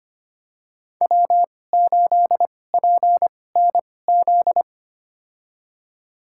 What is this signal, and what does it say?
Morse code, heard as a keyed tone, about 25 wpm, spaced as for 18 wpm, 700 Hz. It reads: W8PNZ
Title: USA Call Signs - 2x Character Spacing - 25wpm